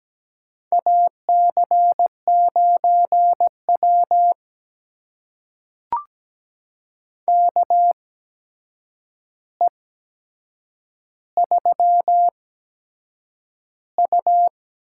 Morse code, heard as a keyed tone, about 17 wpm, spaced as for 7 wpm, 700 Hz. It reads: AC9W KE3U
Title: USA Call Signs - 4 Characters - 8x Character Spacing - 17wpm